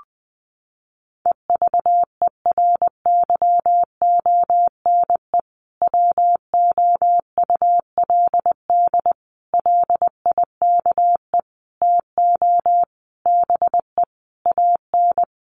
Morse code, heard as a keyed tone, about 20 wpm, 700 Hz. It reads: EVERYONE WOULD LIKE TO BE AN
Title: Sentences from Top 1500 Words 20wpm